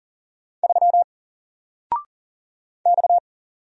Morse code, heard as a keyed tone, about 41 wpm, 700 Hz. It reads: 3 X
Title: Single Letter-Number Alt 40wpm